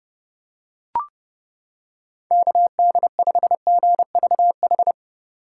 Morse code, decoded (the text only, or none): KD5GVH